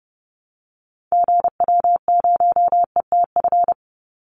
Morse code, heard as a keyed tone, about 30 wpm, 700 Hz. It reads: GW0ETF
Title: Call Signs Repeated 3x - 30wpm